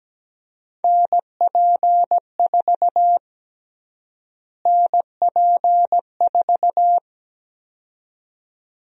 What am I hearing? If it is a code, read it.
Morse code, heard as a keyed tone, about 17 wpm, 700 Hz. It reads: NP4 NP4
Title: USA Call Sign Prefixes - Repeated 2x - 17wpm